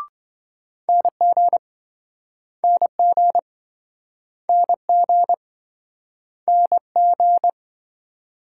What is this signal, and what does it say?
Morse code, heard as a keyed tone, about 26 wpm, 700 Hz. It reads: NG NG NG NG